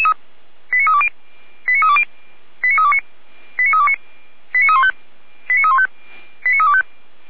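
Five-tone selective call (ZVEI), heard as quick runs of five tones, tones of about 70 ms. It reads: 89320 89320 89329 89329 89315 89315 89325
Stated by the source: ZVEI five-tone call 89503 received on SDR.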